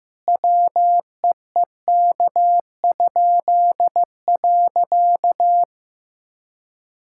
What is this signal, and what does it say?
Morse code, heard as a keyed tone, about 15 wpm, 700 Hz. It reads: WEEK?.
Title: Pangram Challenge v2 - 15wpm